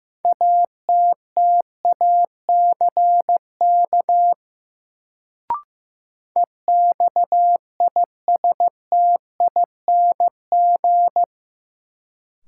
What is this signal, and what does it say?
Morse code, heard as a keyed tone, about 15 wpm, 700 Hz. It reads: ATTACK EXISTING
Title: Sentences from Top 1500 Words - New Word Drill - 15wpm